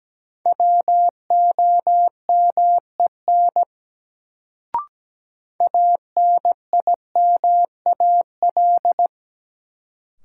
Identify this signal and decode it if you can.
Morse code, heard as a keyed tone, about 17 wpm, 700 Hz. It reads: WOMEN ANIMAL
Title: Sentences from Top 1300 Words - New Word Drill Alt - 17wpm